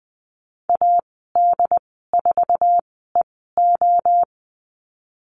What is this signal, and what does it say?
Morse code, heard as a keyed tone, about 20 wpm, spaced as for 14 wpm, 700 Hz. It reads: AD4EO
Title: 069 Lesson 12g - Letter d - Call Signs 20wpm